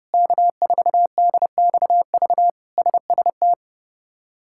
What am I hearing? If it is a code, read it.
Morse code, heard as a keyed tone, about 31 wpm, 700 Hz. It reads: K4DXV SST
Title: K1USN SST QSO Practice - 30wpm